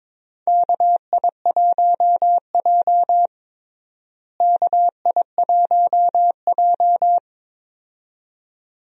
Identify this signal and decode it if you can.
Morse code, heard as a keyed tone, about 22 wpm, 700 Hz. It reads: KI1J KI1J
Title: USA Call Signs - 4 Characters - Repeated 2x - 22wpm